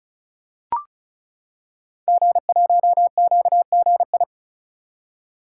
Morse code, heard as a keyed tone, about 36 wpm, 700 Hz. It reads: G1QGI
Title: Code Groups - 5 Letter:Numbers - 35wpm